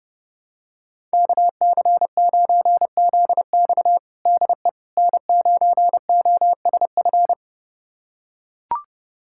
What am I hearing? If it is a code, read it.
Morse code, heard as a keyed tone, about 32 wpm, 700 Hz. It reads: KC9ZX DE N9OSF
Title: USA Callsign DE Callsign - 5 Characters - 30wpm